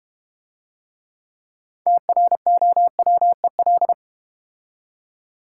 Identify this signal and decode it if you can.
Morse code, heard as a keyed tone, about 30 wpm, 700 Hz. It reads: TROWEL
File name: practice / 25wpm / Difficult Words - Speed-Racing with Hints - 25wpm